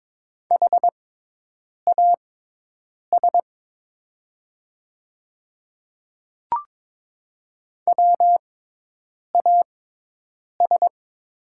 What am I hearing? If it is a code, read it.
Morse code, heard as a keyed tone, about 22 wpm, spaced as for 8 wpm, 700 Hz. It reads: HAS WAS
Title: 6x Character Spacing - Top 100 Words - 22wpm